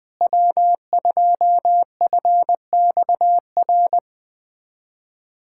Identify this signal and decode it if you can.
Morse code, heard as a keyed tone, about 20 wpm, 700 Hz. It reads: W2FXR
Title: USA Callsign DE Callsign - 5 Characters - 20wpm